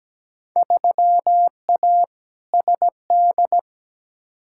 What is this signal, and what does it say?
Morse code, heard as a keyed tone, about 17 wpm, 700 Hz. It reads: 3A SD
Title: Field Day Exchanges - 17wpm